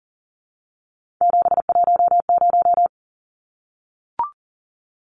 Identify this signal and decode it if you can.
Morse code, heard as a keyed tone, about 40 wpm, 700 Hz. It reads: 710
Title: Numbers - 3-Digit 40wpm